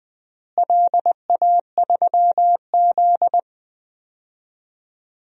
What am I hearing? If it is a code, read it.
Morse code, heard as a keyed tone, about 20 wpm, 700 Hz. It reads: LA3Z
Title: DXCC-100 Call Signs - 4 Characters - Repeated 3x - 20wpm